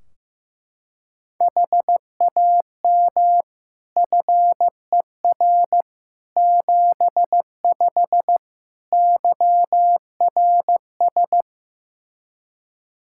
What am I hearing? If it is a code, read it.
Morse code, heard as a keyed tone, about 16 wpm, 700 Hz. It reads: HAM FER 75 YRS
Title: K1USN SST QSO Practice 15wpm